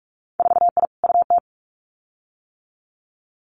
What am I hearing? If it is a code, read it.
Morse code, heard as a keyed tone, about 46 wpm, 700 Hz. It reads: 4I UT
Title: Winter Field Day Exchanges Encore 45wpm